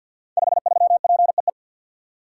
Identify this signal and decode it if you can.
Morse code, heard as a keyed tone, about 46 wpm, 700 Hz. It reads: 53PEE
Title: Code Groups - 5 Letter:Numbers - 50wpm